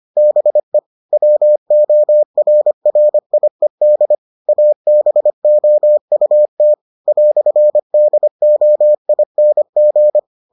Morse code, heard as a keyed tone, about 25 wpm, 600 Hz. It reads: BE WORRIED ABOUT "DOING